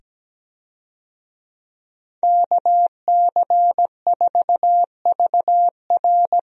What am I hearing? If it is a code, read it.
Morse code, heard as a keyed tone, about 17 wpm, 700 Hz. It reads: KC4VR
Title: USA Call Signs - 5 Characters - 17wpm